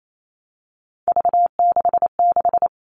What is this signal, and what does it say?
Morse code, heard as a keyed tone, about 28 wpm, 700 Hz. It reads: V66
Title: All Cut Numbers - 3-Digit 28wpm